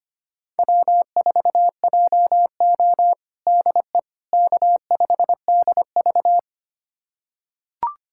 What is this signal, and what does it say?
Morse code, heard as a keyed tone, about 25 wpm, 700 Hz. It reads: W4JO DE K5DV
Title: USA Callsign DE Callsign - 4 Characters - 25wpm